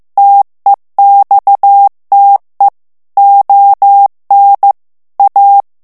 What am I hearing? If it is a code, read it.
Morse code, heard as a keyed tone, about 15 wpm, 800 Hz. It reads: TEXTE ON A
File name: morse mp3 / FRANCAIS TEXTE CLAIR 1 VITESSE 14 MOTS MINUTES